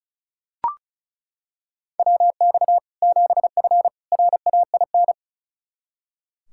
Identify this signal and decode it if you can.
Morse code, heard as a keyed tone, about 36 wpm, 700 Hz. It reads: WX 7F RAIN
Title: Practice QSO - Exchange 2 - Info - 35wpm